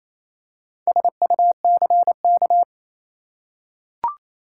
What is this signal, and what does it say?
Morse code, heard as a keyed tone, about 28 wpm, 700 Hz. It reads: SUCK
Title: Four Letter Words - 28wpm